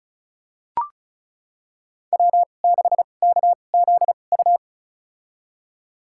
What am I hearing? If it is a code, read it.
Morse code, heard as a keyed tone, about 36 wpm, spaced as for 25 wpm, 700 Hz. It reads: W6KZU
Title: USA Call Signs - 5 Characters - 2x Character Spacing - 35wpm